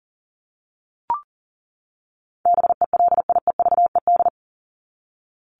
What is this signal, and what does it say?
Morse code, heard as a keyed tone, about 40 wpm, 700 Hz. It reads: BELIEVED
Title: ED Suffix - Top 100 Words with Base Words 40wpm